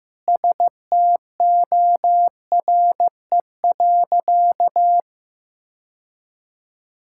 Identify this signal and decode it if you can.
Morse code, heard as a keyed tone, about 15 wpm, 700 Hz. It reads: STORE.
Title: Sentences from Top 1000 Words - Review Part 5 - 15wpm